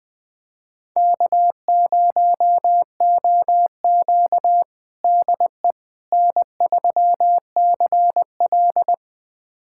Morse code, heard as a keyed tone, about 20 wpm, 700 Hz. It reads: K0OQ DE N3CL
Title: USA Callsign DE Callsign - 4 Characters - 20wpm